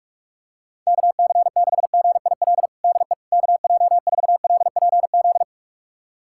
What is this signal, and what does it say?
Morse code, heard as a keyed tone, about 46 wpm, 700 Hz. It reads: KC6GIL DE KJ4LPZ
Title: USA Callsign DE Callsign - 6 Characters - 45wpm